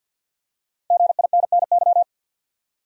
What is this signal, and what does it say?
Morse code, heard as a keyed tone, about 48 wpm, 700 Hz. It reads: GINNY
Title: USA Common Names - 50wpm